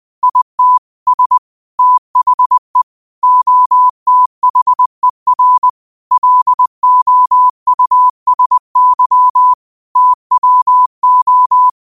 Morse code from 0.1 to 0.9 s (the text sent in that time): IT